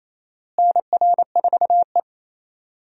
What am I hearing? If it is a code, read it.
Morse code, heard as a keyed tone, about 28 wpm, 700 Hz. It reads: NR4E
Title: USA Call Signs - 4 Characters - Repeated 3x - 28wpm